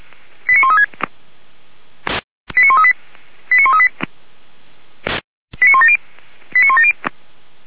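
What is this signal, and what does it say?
Five-tone selective call (ZVEI), heard as quick runs of five tones, tones of about 70 ms. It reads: 89136 89138 89138 89170 89170
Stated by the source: ZVEI five-tone selective call received by SDR, fire brigade alerting, from feuerwehr-orlamuende.org.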